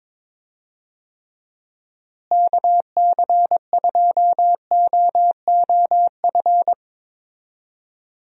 Morse code, heard as a keyed tone, about 22 wpm, 700 Hz. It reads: KC2OOF
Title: USA Call Signs - 6 Characters - Repeated 3x - 22wpm